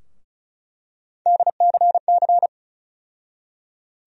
Morse code, heard as a keyed tone, about 36 wpm, 700 Hz. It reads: DCC
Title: USA Call Sign Postfixes - 3 Letters 35wpm